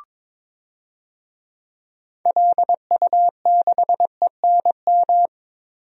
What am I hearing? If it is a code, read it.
Morse code, heard as a keyed tone, about 22 wpm, 700 Hz. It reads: LU6ENM
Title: DXCC-100 Call Signs - 6 Characters - Repeated 3x - 22wpm